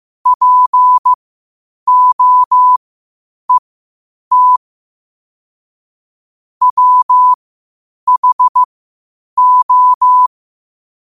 Morse code, heard as a keyed tone, about 15 wpm, spaced as for 8 wpm, 1000 Hz. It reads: POET WHO
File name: qotdfiles / 8wpm